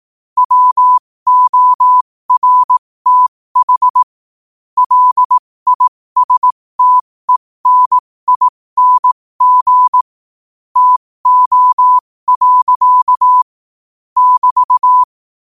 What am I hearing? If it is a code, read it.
Morse code, heard as a keyed tone, about 18 wpm, spaced as for 15 wpm, 1000 Hz. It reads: WORTH LISTENING TO. <BT>